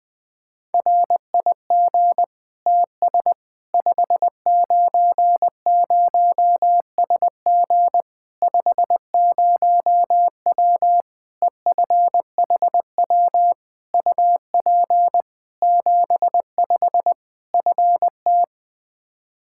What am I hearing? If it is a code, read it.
Morse code, heard as a keyed tone, about 20 wpm, 700 Hz. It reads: RIG TS 590SG 50W EFHW UP 75 FT
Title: Practice QSO - Exchange 2 - Info - 20wpm